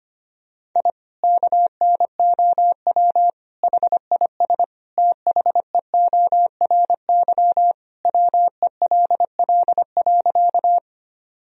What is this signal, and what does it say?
Morse code, heard as a keyed tone, about 25 wpm, 700 Hz. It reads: I KNOW HIS THEORY WELL.
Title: Sentences from Top 1000 Words - Review Part 3 - 25wpm